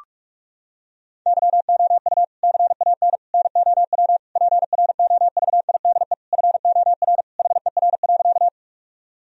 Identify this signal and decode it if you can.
Morse code, heard as a keyed tone, about 41 wpm, 700 Hz. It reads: YOU CAN NOW PROVIDE FOR HER.
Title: Sentences from Top 1000 Words - Review Part 3 - 45wpm